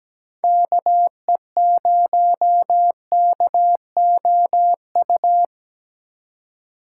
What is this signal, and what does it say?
Morse code, heard as a keyed tone, about 19 wpm, 700 Hz. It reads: KE0KOU
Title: Speed Racing - USA 2x3 Call Signs 17wpm